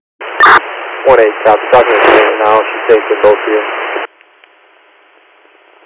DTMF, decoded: none

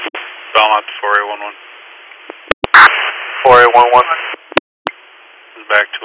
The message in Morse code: none

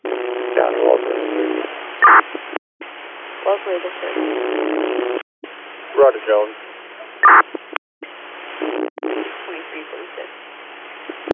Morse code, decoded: none

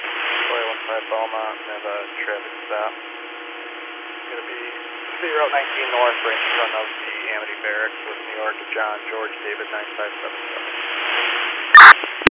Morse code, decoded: none

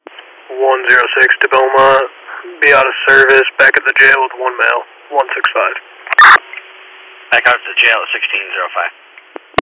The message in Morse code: none

